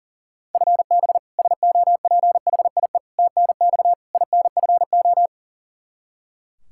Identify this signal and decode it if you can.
Morse code, heard as a keyed tone, about 40 wpm, 700 Hz. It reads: FB SOPHIE TNX INFO
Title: Practice QSO - Exchange 3 - More Info - 40wpm